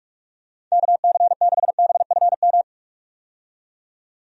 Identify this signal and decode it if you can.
Morse code, heard as a keyed tone, about 42 wpm, 700 Hz. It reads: KC6BFM
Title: USA Call Signs - 6 Characters - 45wpm